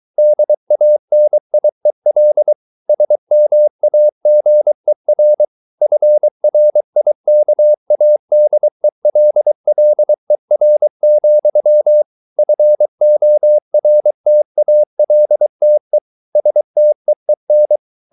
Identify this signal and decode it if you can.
Morse code, heard as a keyed tone, about 23 wpm, 600 Hz. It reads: DANIEL SMAGER FRIKADELLER, FORTALTE STEEN
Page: Ferie 23 wpm